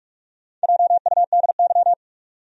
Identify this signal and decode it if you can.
Morse code, heard as a keyed tone, about 45 wpm, 700 Hz. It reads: JUDY
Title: USA Common Names - Max 4 Letters - 45wpm